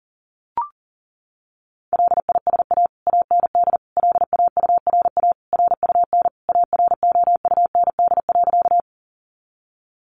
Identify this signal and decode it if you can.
Morse code, heard as a keyed tone, about 40 wpm, 700 Hz. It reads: LISA AND LAURA RUN AROUND.
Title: Additional Easy Phrases 40wpm